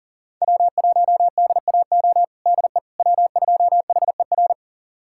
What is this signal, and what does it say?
Morse code, heard as a keyed tone, about 40 wpm, 700 Hz. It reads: W1DAO DE W2HER
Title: USA Callsign DE Callsign - 5 Characters - 40wpm